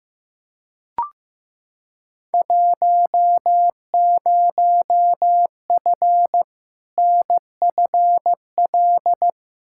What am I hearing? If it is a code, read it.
Morse code, heard as a keyed tone, about 15 wpm, 700 Hz. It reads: 10F NFL
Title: Field Day Exchanges Encore - 15wpm